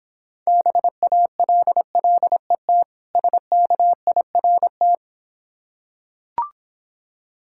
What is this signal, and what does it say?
Morse code, heard as a keyed tone, about 26 wpm, 690 Hz. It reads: BALLET SKIRT